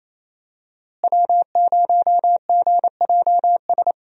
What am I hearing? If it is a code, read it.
Morse code, heard as a keyed tone, about 28 wpm, 700 Hz. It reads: W0GJS
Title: USA Call Signs - 5 Characters - Repeated 3x - 28wpm